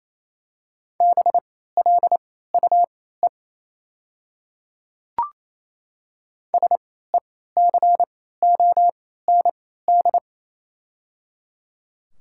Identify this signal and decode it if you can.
Morse code, heard as a keyed tone, about 28 wpm, spaced as for 16 wpm, 700 Hz. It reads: BLUE SECOND
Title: 3x Character Spacing - Top 500 Words - 28wpm